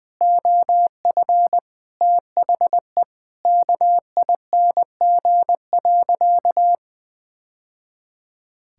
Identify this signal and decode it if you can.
Morse code, heard as a keyed tone, about 20 wpm, 700 Hz. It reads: OF THE KING.
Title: Sentences from Top 1000 Words - Review Part 5 - 20wpm